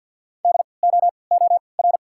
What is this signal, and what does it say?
Morse code, heard as a keyed tone, about 53 wpm, spaced as for 29 wpm, 690 Hz. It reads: DOOR
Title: 3x Character Spacing - Top 300 Words - 50wpm